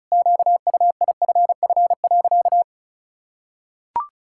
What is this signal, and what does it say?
Morse code, heard as a keyed tone, about 36 wpm, 700 Hz. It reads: QUIFF.
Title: Pangram Challenge v2 - 35wpm